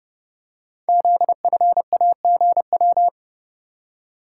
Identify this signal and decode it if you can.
Morse code, heard as a keyed tone, about 30 wpm, 700 Hz. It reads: ZFAGW
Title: Code Groups - 5 Letters - 30wpm